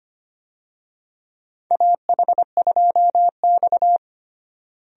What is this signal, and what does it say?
Morse code, heard as a keyed tone, about 25 wpm, 700 Hz. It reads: AH2X
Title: USA Call Signs - 4 Characters - Repeated 2x - 25wpm